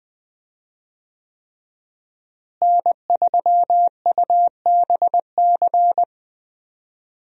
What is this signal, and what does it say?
Morse code, heard as a keyed tone, about 21 wpm, 700 Hz. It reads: N3UBC
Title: USA Call Signs - 5 Characters - Repeated 3x - 20wpm